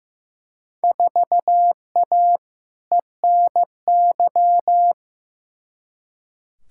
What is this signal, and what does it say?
Morse code, heard as a keyed tone, about 15 wpm, 700 Hz. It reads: 4A ENY
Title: Field Day Exchanges - 15wpm